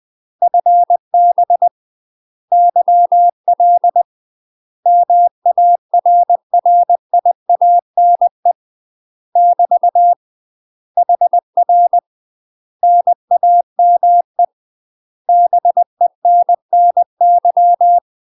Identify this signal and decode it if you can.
Morse code, heard as a keyed tone, about 20 wpm, 700 Hz. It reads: FB YL MARRIANE <BT> HR NAME BENNY
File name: UK MASTER QSOs TRAINING / 20WPM / UK MASTER QSOs 0025